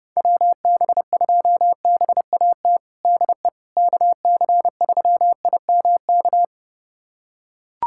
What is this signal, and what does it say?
Morse code, heard as a keyed tone, about 32 wpm, 700 Hz. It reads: WB2BAT DE KC3IMK
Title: USA Callsign DE Callsign - 6 Characters - 30wpm